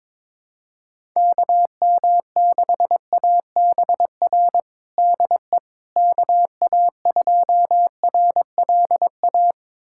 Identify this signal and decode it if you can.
Morse code, heard as a keyed tone, about 22 wpm, 700 Hz. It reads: KM6ABR DE KA2RLA
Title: USA Callsign DE Callsign 22wpm